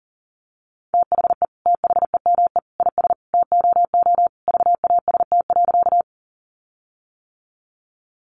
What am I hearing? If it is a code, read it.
Morse code, heard as a keyed tone, about 38 wpm, 700 Hz. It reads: THE THEME IS TOO VAST.
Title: Sentences from Top 1300 Words 40wpm